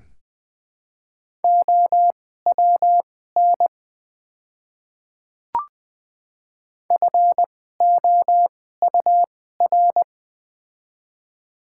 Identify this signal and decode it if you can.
Morse code, heard as a keyed tone, about 20 wpm, spaced as for 14 wpm, 700 Hz. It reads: OWN FOUR
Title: 2x Character Spacing - Top 300 Words - 20wpm